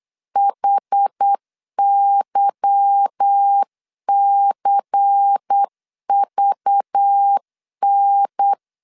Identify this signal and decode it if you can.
Morse code, heard as a keyed tone, about 8 wpm, 800 Hz. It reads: HYCVN
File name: LEZIONI / 9 28 8wpm 3512